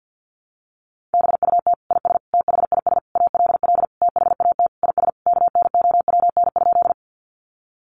Morse code, heard as a keyed tone, about 49 wpm, 700 Hz. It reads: BUT IS THIS ALL THAT IS KNOWN?
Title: Sentences from Top 600 Words 50wpm